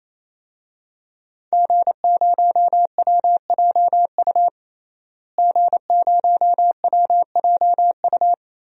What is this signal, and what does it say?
Morse code, heard as a keyed tone, about 28 wpm, 700 Hz. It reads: G0WJU G0WJU
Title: DXCC-100 Call Signs - 5 Characters - Repeated 3x - 28wpm